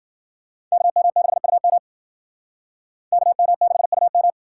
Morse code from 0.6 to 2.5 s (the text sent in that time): KM6UM